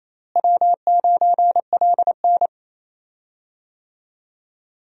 Morse code, heard as a keyed tone, about 28 wpm, 700 Hz. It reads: W9LN
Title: USA Call Signs - 4 Characters - Repeated 3x - 28wpm